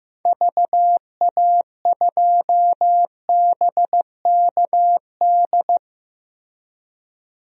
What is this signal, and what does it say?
Morse code, heard as a keyed tone, about 15 wpm, 700 Hz. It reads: VA2BKD
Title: DXCC-100 Call Signs - 6 Characters - Repeated 2x - 15wpm